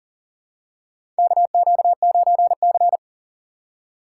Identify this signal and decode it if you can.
Morse code, heard as a keyed tone, about 40 wpm, 700 Hz. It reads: KQ9C
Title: USA Call Signs - 4 Characters - 2x Character Spacing - 40wpm